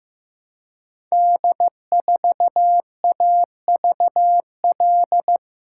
Morse code, heard as a keyed tone, about 15 wpm, 700 Hz. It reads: D4AVL
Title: Code Groups - 5 Letter:Numbers - 15wpm